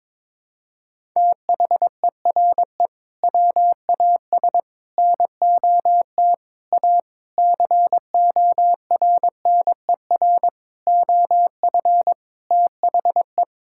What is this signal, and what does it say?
Morse code, heard as a keyed tone, about 22 wpm, 700 Hz. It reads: THERE WAS NOT A CORNER OF THE
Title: Sentences from Top 1600 Words 22wpm